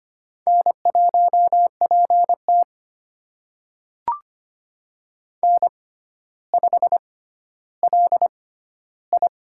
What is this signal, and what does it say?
Morse code, heard as a keyed tone, about 25 wpm, spaced as for 13 wpm, 700 Hz. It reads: N1PT N5LI
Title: USA Call Signs - 6x Character Spacing - 25wpm